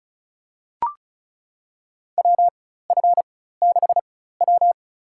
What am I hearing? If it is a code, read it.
Morse code, heard as a keyed tone, about 36 wpm, spaced as for 16 wpm, 700 Hz. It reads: WF6W
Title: USA Call Signs - 4x Character Spacing - 35wpm